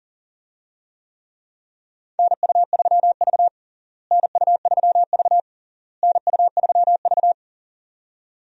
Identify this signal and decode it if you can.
Morse code, heard as a keyed tone, about 37 wpm, 700 Hz. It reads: NU3V NU3V NU3V
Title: USA Call Signs - 4 Characters - Repeated 3x - 40wpm